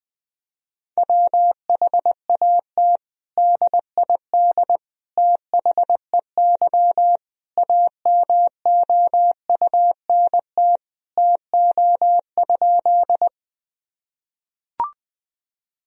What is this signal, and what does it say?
Morse code, heard as a keyed tone, about 20 wpm, 700 Hz. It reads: WHAT DID THEY AMOUNT TO?
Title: Sentences from Top 1000 Words - Review Part 4 - 20wpm